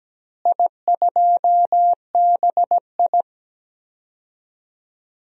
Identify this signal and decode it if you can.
Morse code, heard as a keyed tone, about 17 wpm, 700 Hz. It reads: I2BI
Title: DXCC-100 Call Signs - 4 Characters - Repeated 2x - 17wpm